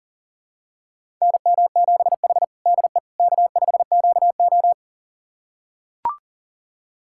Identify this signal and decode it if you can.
Morse code, heard as a keyed tone, about 40 wpm, 700 Hz. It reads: NM7H DE K5QO
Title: USA Callsign DE Callsign - 4 Characters - 40wpm